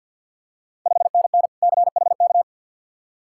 Morse code, heard as a keyed tone, about 51 wpm, 690 Hz. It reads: 5NN CHK
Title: Ontario QSO Party Exchange 50wpm